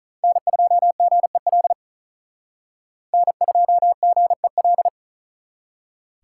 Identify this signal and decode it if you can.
Morse code, heard as a keyed tone, about 45 wpm, 700 Hz. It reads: N2GEL N2GEL